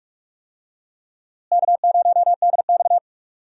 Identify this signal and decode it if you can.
Morse code, heard as a keyed tone, about 47 wpm, 700 Hz. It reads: K0DX
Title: USA Call Signs - 4 Characters - Repeated 2x - 45wpm